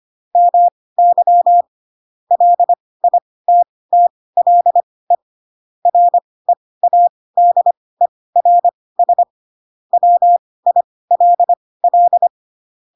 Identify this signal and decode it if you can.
Morse code, heard as a keyed tone, about 25 wpm, spaced as for 18 wpm, 700 Hz. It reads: MY LITTLE READERS WILL
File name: Pinocchio / Pinocchio ch1 12 25 ik1hge